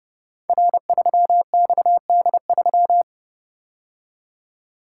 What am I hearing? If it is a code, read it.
Morse code, heard as a keyed tone, about 30 wpm, 700 Hz. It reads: R3XD3
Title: Code Groups - 5 Letter:Numbers - 30wpm